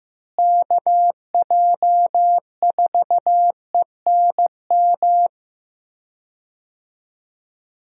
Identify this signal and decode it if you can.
Morse code, heard as a keyed tone, about 15 wpm, 700 Hz. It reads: KJ4ENM
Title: USA Call Signs - 6 Characters - Repeated 2x - 15wpm